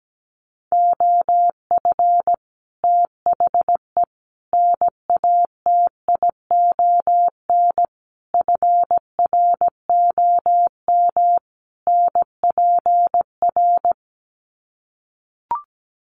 Morse code, heard as a keyed tone, about 17 wpm, 700 Hz. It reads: OF THE NATION FROM NPR
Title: Sets of 5 Words 17wpm